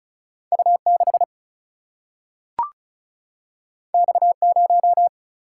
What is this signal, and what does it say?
Morse code, heard as a keyed tone, about 36 wpm, 700 Hz. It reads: U 6 X 0
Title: Single-Letter Single-Number 35wpm